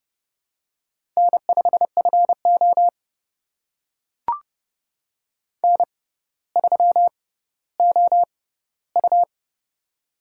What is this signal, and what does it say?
Morse code, heard as a keyed tone, about 30 wpm, spaced as for 15 wpm, 700 Hz. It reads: N5FO N3OU
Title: USA Call Signs - 4 Characters - 6x Character Spacing - 30wpm